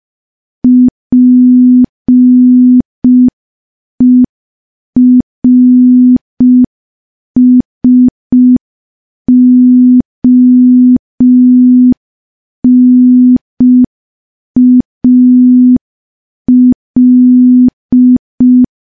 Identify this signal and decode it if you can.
Morse code, heard as a keyed tone, about 5 wpm, 250 Hz. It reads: PERSONAL
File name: morse / morse-training-05WPM